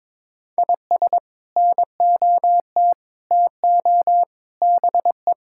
Morse code, heard as a keyed tone, about 22 wpm, 700 Hz. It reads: IS NOT TO BE
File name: practice / 22wpm / Sets of 7 Words - 22wpm